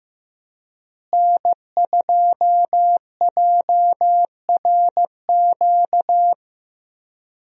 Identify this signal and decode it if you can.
Morse code, heard as a keyed tone, about 15 wpm, 700 Hz. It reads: N2JRQ
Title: USA Call Signs - 5 Characters - Repeated 3x - 15wpm